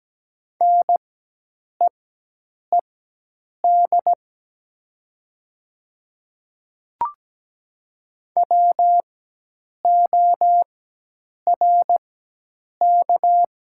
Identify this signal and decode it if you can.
Morse code, heard as a keyed tone, about 17 wpm, spaced as for 8 wpm, 700 Hz. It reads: NEED WORK
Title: 4x Character Spacing - Top 300 Words - 17wpm